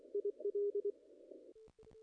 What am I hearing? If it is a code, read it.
Morse code, heard as a keyed tone, about 24 wpm, 410 Hz. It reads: IL
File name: contests / ss-cw-k5tr-op-n5rz -0100